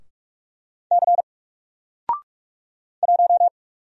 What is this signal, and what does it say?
Morse code, heard as a keyed tone, about 44 wpm, 700 Hz. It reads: C 1
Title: Single Character (Letter:Number:Punctuation:Prosign) Alt 45wpm